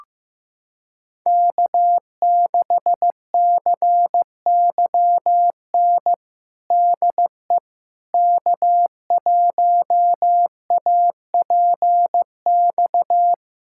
Morse code, heard as a keyed tone, about 15 wpm, 700 Hz. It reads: K6CYN DE K1APX
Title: USA Callsign DE Callsign - 5 Characters - 15wpm